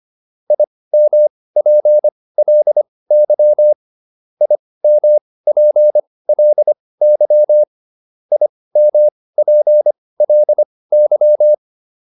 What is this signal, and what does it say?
Morse code, heard as a keyed tone, about 25 wpm, spaced as for 18 wpm, 600 Hz. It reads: IMPLY IMPLY IMPLY
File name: IM5 18